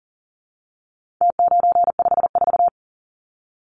Morse code, heard as a keyed tone, about 40 wpm, 700 Hz. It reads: T954
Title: Numbers - Contest Style - 4-Digit 40wpm